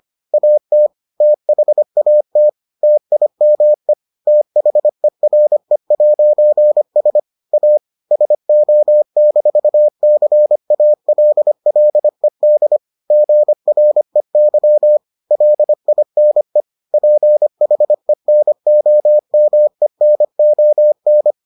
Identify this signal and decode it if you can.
Morse code, heard as a keyed tone, about 25 wpm, 600 Hz. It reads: AT THAT TIME THERE'S A SO-CALLED GREY LINE PHENOMENON